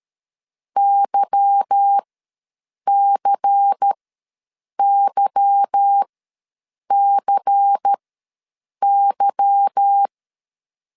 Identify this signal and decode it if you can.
Morse code, heard as a keyed tone, about 13 wpm, spaced as for 7 wpm, 800 Hz. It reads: YCYCY
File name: LEZIONI / 8 YJ45 / 8 Y ws 12wpm 3409